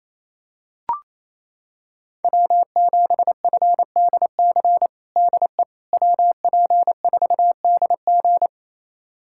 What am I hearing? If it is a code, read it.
Morse code, heard as a keyed tone, about 28 wpm, 700 Hz. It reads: W7FDC DE WP4DG
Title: USA Callsign DE Callsign - 5 Characters - 28wpm